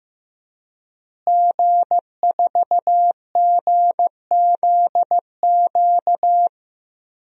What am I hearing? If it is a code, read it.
Morse code, heard as a keyed tone, about 15 wpm, 700 Hz. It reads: G4GZQ
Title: DXCC-100 Call Signs - 5 Characters - Repeated 3x - 15wpm